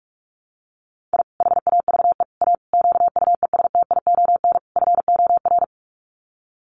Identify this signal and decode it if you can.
Morse code, heard as a keyed tone, about 44 wpm, 700 Hz. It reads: I HAVE A QUESTION FOR
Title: Sets of 5 Words Encore2 45wpm